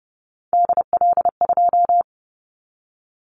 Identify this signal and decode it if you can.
Morse code, heard as a keyed tone, about 30 wpm, 700 Hz. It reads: DL2
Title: Callsign Prefixes for Top 25 DXCC - 30wpm